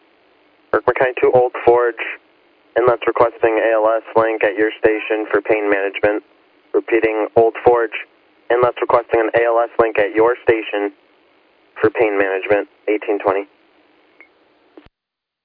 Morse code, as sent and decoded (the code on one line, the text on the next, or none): none